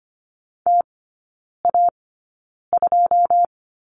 Morse code, heard as a keyed tone, about 25 wpm, spaced as for 9 wpm, 700 Hz. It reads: TA2
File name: Lesson 14e - Number 2 - Three Characters 25wpm